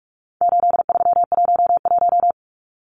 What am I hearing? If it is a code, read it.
Morse code, heard as a keyed tone, about 44 wpm, 700 Hz. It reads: 8311
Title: Numbers - 4-Digit 45wpm